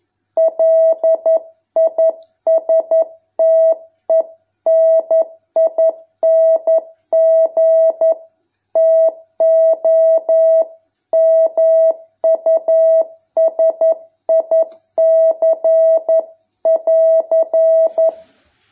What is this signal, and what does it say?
Morse code, heard as a keyed tone, about 11 wpm, 650 Hz. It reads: LISTENING TO MUSIC<AR>